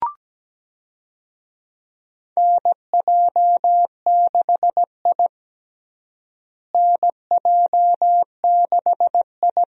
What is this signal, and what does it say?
Morse code, heard as a keyed tone, about 17 wpm, 700 Hz. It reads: NJ6I NJ6I
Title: USA Call Signs - 4 Characters - Repeated 2x - 17wpm